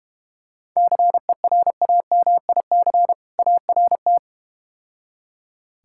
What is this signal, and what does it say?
Morse code, heard as a keyed tone, about 33 wpm, 700 Hz. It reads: CERAMIC ART